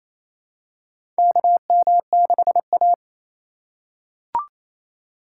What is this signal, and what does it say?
Morse code, heard as a keyed tone, about 28 wpm, 700 Hz. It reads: KM6A
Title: USA Call Signs - 4 Characters - 3x Character Spacing - 28wpm